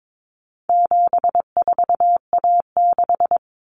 Morse code, heard as a keyed tone, about 22 wpm, 700 Hz. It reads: 74A6
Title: Common Cut Numbers - 4-Digit 22wpm